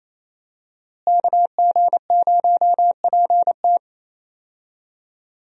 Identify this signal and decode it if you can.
Morse code, heard as a keyed tone, about 28 wpm, 700 Hz. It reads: KG0PT
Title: USA Call Signs - 5 Characters - Repeated 3x - 28wpm